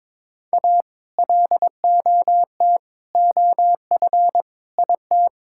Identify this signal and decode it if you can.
Morse code, heard as a keyed tone, about 21 wpm, 700 Hz. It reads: A LOT OF IT
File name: Sentences with 9-Word Phrases - 22wpm